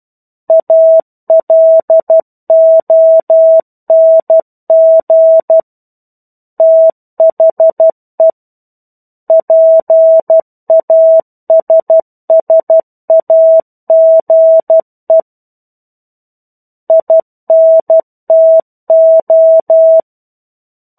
Morse code, heard as a keyed tone, about 12 wpm, 650 Hz. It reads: ALONG THE PASSAGE INTO
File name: 01-Chapter-01 12 WPM